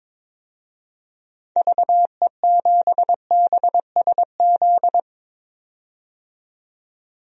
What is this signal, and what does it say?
Morse code, heard as a keyed tone, about 22 wpm, 700 Hz. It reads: VE7BSZ
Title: DXCC-100 Call Signs - 6 Characters - 22wpm